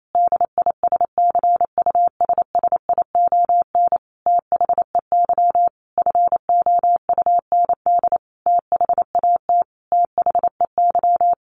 Morse code, heard as a keyed tone, about 28 wpm, 700 Hz. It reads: DISCUSSION THEY FOUND THAT THEY
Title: Sentences from Top 1200 Words 28wpm